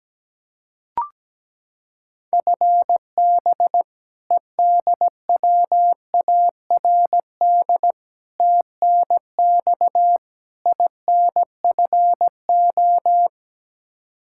Morse code, heard as a keyed tone, about 17 wpm, 700 Hz. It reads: FB EDWARD TNX INFO
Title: Practice QSO - Exchange 3 - More Info - 17wpm